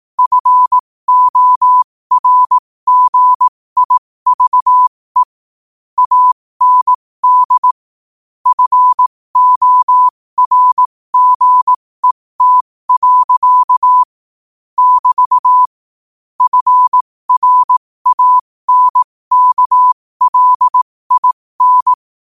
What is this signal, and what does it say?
Morse code, heard as a keyed tone, about 18 wpm, spaced as for 15 wpm, 1000 Hz. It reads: FORGIVE AND FORGET. <BT> FRANKLIN
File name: qotdfiles / 15wpm